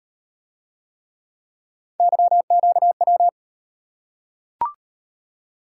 Morse code, heard as a keyed tone, about 38 wpm, 700 Hz. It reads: YQW